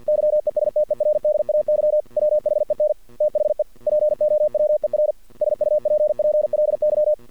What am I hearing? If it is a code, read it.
Morse code, heard as a keyed tone, about 48 wpm, 600 Hz. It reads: YESTERDAY WHEN THE POWER SUPPLY
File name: archived / CRSnet